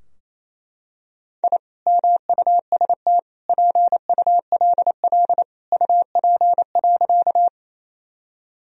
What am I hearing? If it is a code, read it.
Morse code, heard as a keyed tone, about 28 wpm, 700 Hz. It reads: I MUST PULL UP.
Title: Sentences from Top 1000 Words - Review Part 2 - 28wpm